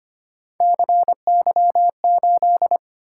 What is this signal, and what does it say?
Morse code, heard as a keyed tone, about 25 wpm, 700 Hz. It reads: CY8
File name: Three Letter-Number Combinations - 25wpm